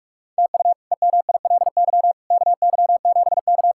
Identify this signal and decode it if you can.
Morse code, heard as a keyed tone, about 46 wpm, 700 Hz. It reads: TU EMILY KY7K
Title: K1USN SST QSO Practice - 45wpm